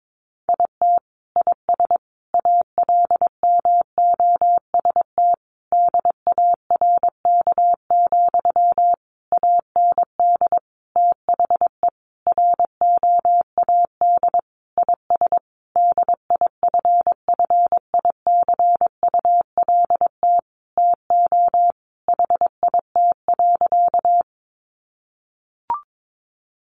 Morse code, heard as a keyed tone, about 22 wpm, 700 Hz. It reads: IT IS ALMOST DARK, AND THE ROAD IS DIFFICULT TO HIT.
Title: Sentences from Top 1200 Words 22wpm